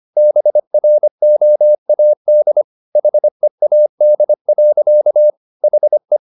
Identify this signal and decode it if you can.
Morse code, heard as a keyed tone, about 25 wpm, 600 Hz. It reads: BROAD HEAD. HE